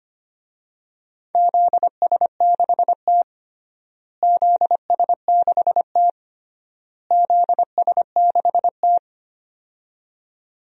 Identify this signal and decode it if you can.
Morse code, heard as a keyed tone, about 25 wpm, 700 Hz. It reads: ZS6T ZS6T ZS6T
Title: DXCC-100 Call Signs - 4 Characters - Repeated 3x - 25wpm